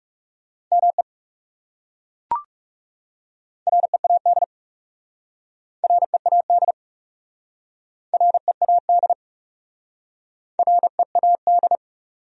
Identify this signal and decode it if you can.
Morse code, heard as a keyed tone, about 37 wpm, 700 Hz. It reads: ME READ READ READ READ
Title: Speed-Racing Top 300 Words 30wpm